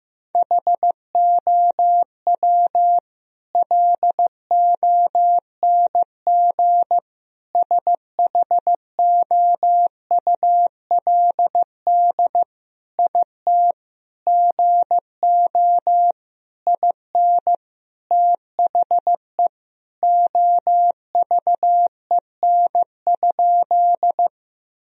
Morse code, heard as a keyed tone, about 15 wpm, 700 Hz. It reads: HOW LONG SHOULD IT GO IN THE OVEN?
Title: Sentences from Top 1000 Words - Review Part 4 - 15wpm